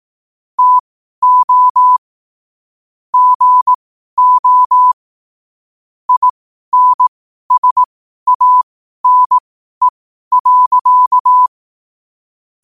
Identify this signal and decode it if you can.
Morse code, heard as a keyed tone, about 18 wpm, spaced as for 12 wpm, 1000 Hz. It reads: TO GO INSANE.
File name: qotdfiles / 12wpm